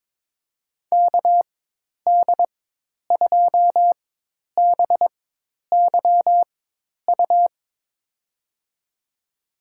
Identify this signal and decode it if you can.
Morse code, heard as a keyed tone, about 23 wpm, spaced as for 10 wpm, 700 Hz. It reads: KD2BYU
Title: USA Call Signs - 6 Characters - 4x Character Spacing - 22wpm